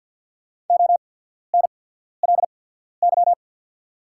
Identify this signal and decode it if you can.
Morse code, heard as a keyed tone, about 48 wpm, spaced as for 14 wpm, 700 Hz. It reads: ONLY
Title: 8x Character Spacing - Top 300 Words - 50wpm